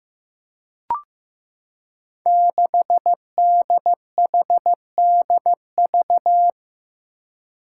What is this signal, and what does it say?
Morse code, heard as a keyed tone, about 15 wpm, 700 Hz. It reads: 6DHDV
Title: Five Challenging Letter-Number Combinations 15wpm